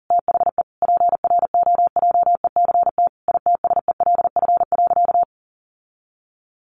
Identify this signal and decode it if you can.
Morse code, heard as a keyed tone, about 40 wpm, 700 Hz. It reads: THE PROJECT ITSELF.
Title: Sentences from Top 1400 Words 40wpm